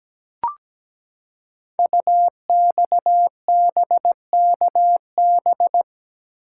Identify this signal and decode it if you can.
Morse code, heard as a keyed tone, about 17 wpm, 700 Hz. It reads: UXBKB
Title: Code Groups - 5 Letters - 17wpm